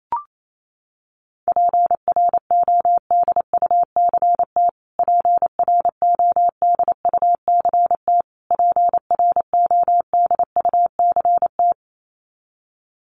Morse code, heard as a keyed tone, about 28 wpm, 700 Hz. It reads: PRODUCT PRODUCT PRODUCT
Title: Top 500 Words Repeated 28wpm